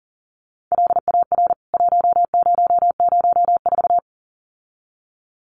Morse code, heard as a keyed tone, about 40 wpm, 700 Hz. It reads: LAR 1004
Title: CWT Exchange 40wpm